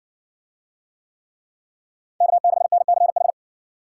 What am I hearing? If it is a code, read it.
Morse code, heard as a keyed tone, about 54 wpm, 700 Hz. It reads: K6NCH